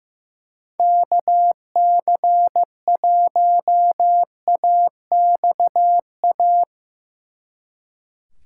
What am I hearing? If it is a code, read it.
Morse code, heard as a keyed tone, about 17 wpm, 700 Hz. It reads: KC1AXA
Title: Speed Racing - USA 2x3 Call Signs 15wpm